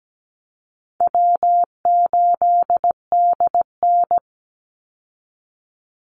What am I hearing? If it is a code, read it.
Morse code, heard as a keyed tone, about 17 wpm, 700 Hz. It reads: W8DN
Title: Easy Call Signs Repeated 3x - 17wpm